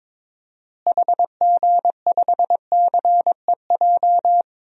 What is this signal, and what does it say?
Morse code, heard as a keyed tone, about 22 wpm, 700 Hz. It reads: HG5CEJ
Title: DXCC-100 Call Signs - 6 Characters - Repeated 3x - 22wpm